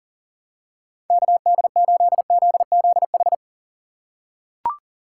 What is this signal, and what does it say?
Morse code, heard as a keyed tone, about 37 wpm, 700 Hz. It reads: KD8ZZH
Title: USA Call Signs - 6 Characters - 6x Character Spacing - 40wpm